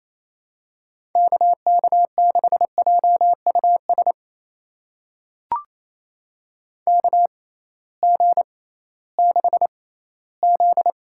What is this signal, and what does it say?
Morse code, heard as a keyed tone, about 28 wpm, spaced as for 16 wpm, 700 Hz. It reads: KK6JUS KG6Z
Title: USA Call Signs - 6 Characters - 6x Character Spacing - 28wpm